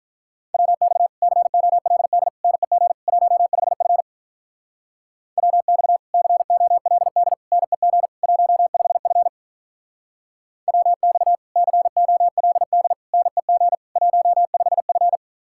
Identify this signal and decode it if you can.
Morse code, heard as a keyed tone, about 44 wpm, 700 Hz. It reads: WX COLD NEG 15F WX COLD NEG 15F WX COLD NEG 15F